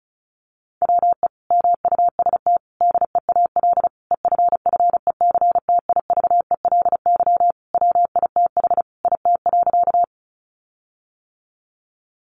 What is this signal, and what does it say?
Morse code, heard as a keyed tone, about 34 wpm, 700 Hz. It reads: WE MUST DEAL EFFECTIVELY WITH IT.
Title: Sentences from Top 1400 Words 35wpm